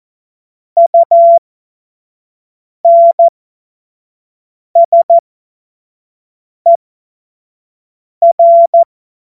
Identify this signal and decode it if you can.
Morse code, heard as a keyed tone, about 14 wpm, spaced as for 5 wpm, 680 Hz. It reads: UNSER
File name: Morsekurs / Witze / Witz 05 Dezember 2023 25 F70